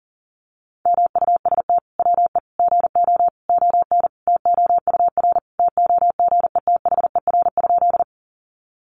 Morse code, heard as a keyed tone, about 38 wpm, 700 Hz. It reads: MUST WE GO ON TOUR TOGETHER?
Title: Sentences from Top 1400 Words 40wpm